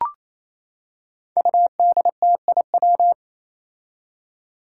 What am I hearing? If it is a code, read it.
Morse code, heard as a keyed tone, about 28 wpm, 700 Hz. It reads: UDTIW
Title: Five Challenging Letter-Number Combinations 28wpm